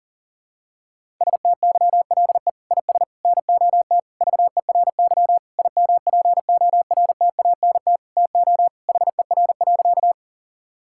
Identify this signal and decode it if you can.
Morse code, heard as a keyed tone, about 40 wpm, 700 Hz. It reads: STYLE IS NOT VERY IMPORTANT TO HER.
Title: Sentences from Top 1000 Words - Review Part 5 - 40wpm